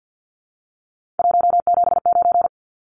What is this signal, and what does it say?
Morse code, heard as a keyed tone, about 50 wpm, 700 Hz. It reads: 179
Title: Numbers - 3-Digit 50wpm